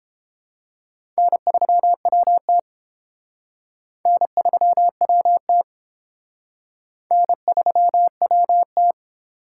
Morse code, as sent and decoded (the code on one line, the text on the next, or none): -. ...-- .-- - / -. ...-- .-- - / -. ...-- .-- -
N3WT N3WT N3WT